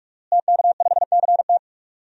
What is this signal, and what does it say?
Morse code, heard as a keyed tone, about 46 wpm, 700 Hz. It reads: TK5CT
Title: DXCC-100 Call Signs - 5 Characters - Repeated 2x - 45wpm